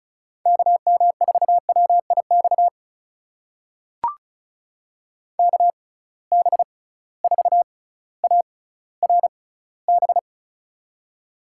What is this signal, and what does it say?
Morse code, heard as a keyed tone, about 34 wpm, spaced as for 18 wpm, 700 Hz. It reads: KM4WIX KB4ARB
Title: USA Call Signs - 6 Characters - 6x Character Spacing - 35wpm